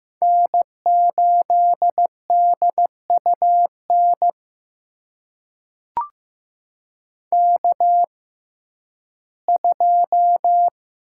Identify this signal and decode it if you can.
Morse code, heard as a keyed tone, about 15 wpm, spaced as for 11 wpm, 700 Hz. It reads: N8DUN K2
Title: USA Call Signs - 5 Characters - 6x Character Spacing - 15wpm